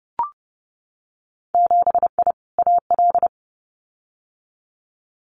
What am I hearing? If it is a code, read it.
Morse code, heard as a keyed tone, about 30 wpm, 700 Hz. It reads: 7I AL
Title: Winter Field Day Exchanges Encore 30wpm